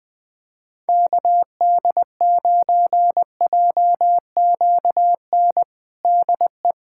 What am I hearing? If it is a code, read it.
Morse code, heard as a keyed tone, about 21 wpm, 700 Hz. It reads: KD9JQN DE
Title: K1USN SST QSO Practice 20wpm